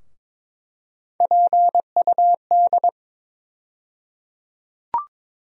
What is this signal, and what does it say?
Morse code, heard as a keyed tone, about 21 wpm, 700 Hz. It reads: PUD
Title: USA Call Sign Postfixes - 3 Letters 22wpm